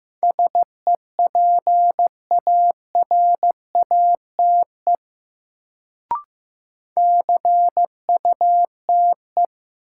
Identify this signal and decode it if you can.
Morse code, heard as a keyed tone, about 15 wpm, 700 Hz. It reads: SEPARATE CUTE
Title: Sentences from Top 1000 Words - Review Part 7 - New Word Drill - 15wpm